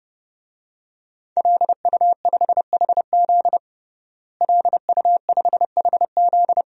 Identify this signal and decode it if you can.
Morse code, heard as a keyed tone, about 32 wpm, 700 Hz. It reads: LU5HZ LU5HZ
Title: DXCC-100 Call Signs - 5 Characters - Repeated 2x - 30wpm